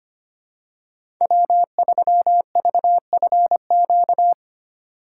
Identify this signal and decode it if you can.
Morse code, heard as a keyed tone, about 25 wpm, 700 Hz. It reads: W3VFQ
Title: USA Call Signs - 5 Characters - 2x Character Spacing - 25wpm